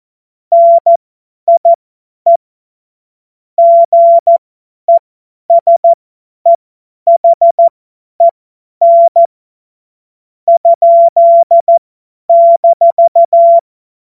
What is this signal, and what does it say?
Morse code, heard as a keyed tone, about 14 wpm, spaced as for 10 wpm, 680 Hz. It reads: NIE GESEHEN ?-
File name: Morsekurs / Witze / Witz 04. Oktober 2022 50 F70